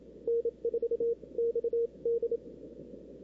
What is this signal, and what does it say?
Morse code, heard as a keyed tone, about 27 wpm, spaced as for 23 wpm, 470 Hz. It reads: N4XD
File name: CW / 2005-k5tr arrl cw dx -03:00